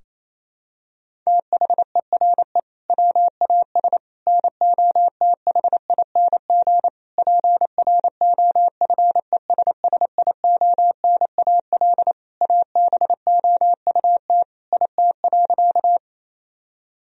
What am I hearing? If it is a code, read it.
Morse code, heard as a keyed tone, about 28 wpm, 700 Hz. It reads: THERE WAS NOTHING PROFESSIONAL ABOUT IT.
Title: Sentences from Top 1000 Words - Review Part 4 - 28wpm